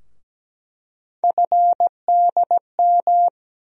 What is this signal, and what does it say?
Morse code, heard as a keyed tone, about 17 wpm, 700 Hz. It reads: FDM
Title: USA Call Sign Postfixes - 3 Letters 17wpm